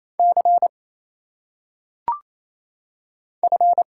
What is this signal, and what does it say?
Morse code, heard as a keyed tone, about 29 wpm, 700 Hz. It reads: C F
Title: Single Letters - Phonetic Pronunciation 28wpm